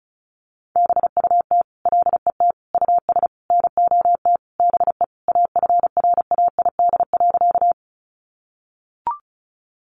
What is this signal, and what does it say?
Morse code, heard as a keyed tone, about 34 wpm, 700 Hz. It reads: BUT LET US NOT BE AFRAID.
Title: Sentences from Top 1500 Words 35wpm